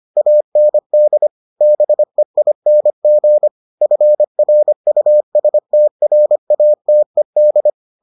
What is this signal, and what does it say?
Morse code, heard as a keyed tone, about 25 wpm, 600 Hz. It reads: AND BEING FRUSTRATED